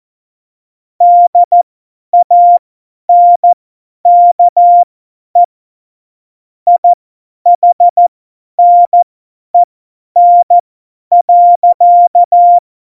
Morse code, heard as a keyed tone, about 14 wpm, spaced as for 10 wpm, 700 Hz. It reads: DANKE IHNEN.
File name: Morsekurs / Witze / Witz 11 Juli 17, Tempo 50 F70